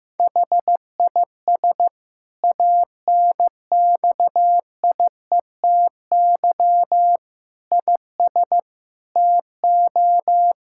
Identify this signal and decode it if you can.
Morse code, heard as a keyed tone, about 15 wpm, 700 Hz. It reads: HIS ANXIETY IS TO
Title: Sentences from Top 1600 Words 15wpm